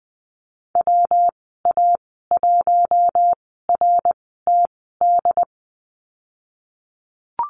Morse code, heard as a keyed tone, about 20 wpm, spaced as for 15 wpm, 700 Hz. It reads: WA1RTD
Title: 161 Lesson 26g - Letter g - Call Signs 20wpm